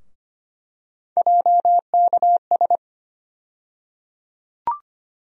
Morse code, heard as a keyed tone, about 25 wpm, 700 Hz. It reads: JKS
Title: USA Call Sign Postfixes - 3 Letters 25wpm